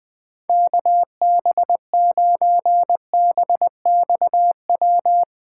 Morse code, heard as a keyed tone, about 20 wpm, 700 Hz. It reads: KB9BXW
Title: USA Call Signs - 6 Characters - 4x Character Spacing - 20wpm